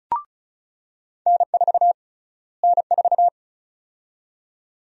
Morse code, heard as a keyed tone, about 34 wpm, 700 Hz. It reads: N4 N4
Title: USA Call Sign Prefixes - Repeated 2x - 35wpm